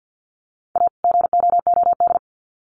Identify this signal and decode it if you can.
Morse code, heard as a keyed tone, about 50 wpm, 700 Hz. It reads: A GOOD
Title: Sets of 2 Words 50wpm